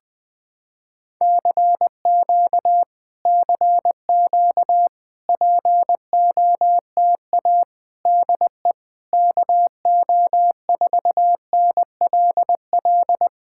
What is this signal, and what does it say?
Morse code, heard as a keyed tone, about 20 wpm, 700 Hz. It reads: CQ CQ POTA DE KO4NLL
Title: POTA QSO Practice - 20wpm